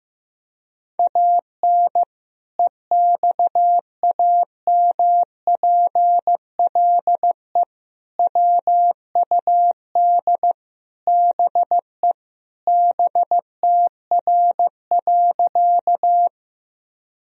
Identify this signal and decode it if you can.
Morse code, heard as a keyed tone, about 16 wpm, 700 Hz. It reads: AN EXAMPLE WUD BE BTR.
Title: Sentences from Top 400 Words and Abbreviations 15wpm